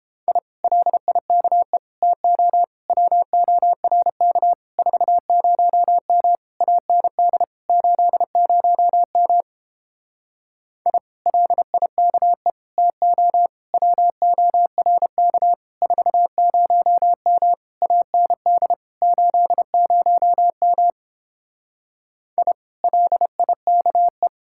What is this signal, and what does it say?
Morse code, heard as a keyed tone, about 31 wpm, 700 Hz. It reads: I LIKE TO WORK 40M AND 80M I LIKE TO WORK 40M AND 80M I LIKE